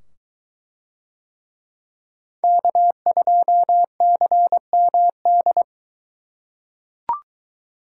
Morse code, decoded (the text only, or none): K2CMD